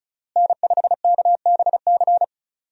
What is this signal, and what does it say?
Morse code, heard as a keyed tone, about 36 wpm, 700 Hz. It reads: N5KBC
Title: USA Call Signs - 5 Characters - Repeated 3x - 35wpm